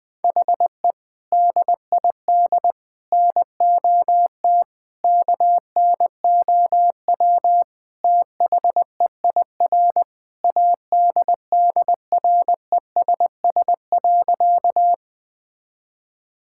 Morse code, heard as a keyed tone, about 20 wpm, 700 Hz. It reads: HE DID NOT KNOW THEIR ADDRESS.
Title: Sentences from Top 1000 Words - Review Part 6 - 20wpm